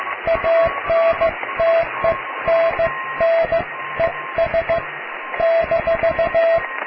Morse code, heard as a keyed tone, about 15 wpm, 660 Hz, with radio noise. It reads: ANTENNES -
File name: cw / tekst(a)+QRM